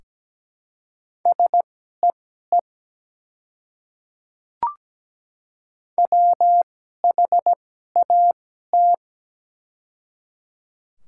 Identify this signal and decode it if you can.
Morse code, heard as a keyed tone, about 17 wpm, spaced as for 12 wpm, 690 Hz. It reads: SEE WHAT
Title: 2x Character Spacing - Top 100 Words - 17wpm